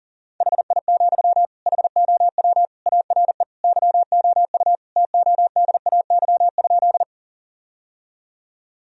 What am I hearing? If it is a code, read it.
Morse code, heard as a keyed tone, about 40 wpm, 700 Hz. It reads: HI, HOW ARE YOU TODAY?
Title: Sentences from Top 1000 Words - Review Part 3 - 40wpm